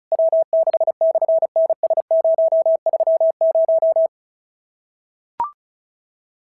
Morse code, heard as a keyed tone, about 34 wpm, 650 Hz. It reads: W6/NS030
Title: SOTA QSO Practice 35wpm